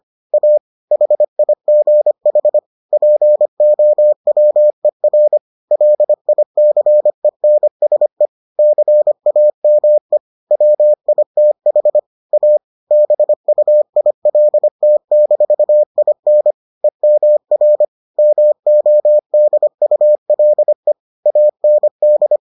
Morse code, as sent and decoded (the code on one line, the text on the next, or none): .- / .... .. --. .... / .--. --- .-- . .-. / .-.. .. -.-. -. ... / -.-. .- -- . / .-- .. - .... / .- / -... ..- .. .-.. - -....- .. -. / . -- .-. / -- --- -.. ..- .-.. / .- -. -..
A HIGH POWER LICENSE CAME WITH A BUILT-IN EMR MODULE AND